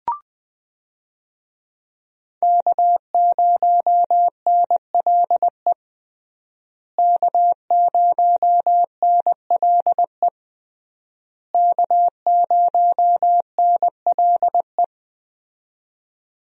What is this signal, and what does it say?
Morse code, heard as a keyed tone, about 20 wpm, 700 Hz. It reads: K0NLE K0NLE K0NLE
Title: USA Call Signs - 5 Characters - Repeated 3x - 20wpm